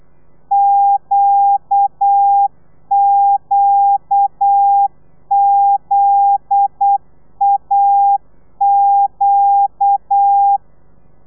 Morse code, heard as a keyed tone, about 8 wpm, 800 Hz. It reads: QQZAQ
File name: telegrafia / i2tfj / Lez9 35